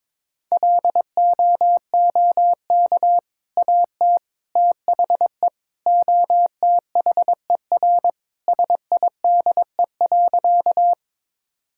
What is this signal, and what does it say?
Morse code, heard as a keyed tone, about 22 wpm, 700 Hz. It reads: LOOK AT THE OTHER SIDE.
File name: practice / 22wpm / Sentences with 2-Word Phrases - 22wpm